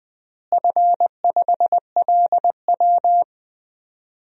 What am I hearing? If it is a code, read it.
Morse code, heard as a keyed tone, about 20 wpm, 700 Hz. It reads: F5LW
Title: DXCC-100 Call Signs - 4 Characters - Repeated 3x - 20wpm